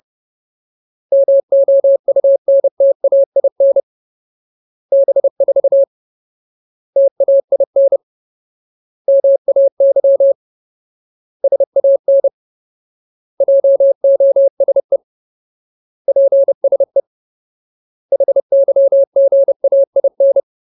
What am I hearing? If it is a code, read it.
Morse code, heard as a keyed tone, about 30 wpm, 550 Hz. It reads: MOUNTAIN B4 TAIN MAY SAN JOSE PSE HYGAIN
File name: Ultimate Word List / 30WPM Ultimate